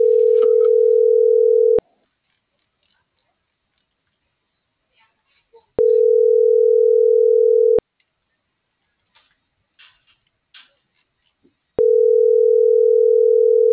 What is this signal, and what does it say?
Ringback tone (North America), 440 + 480 Hz.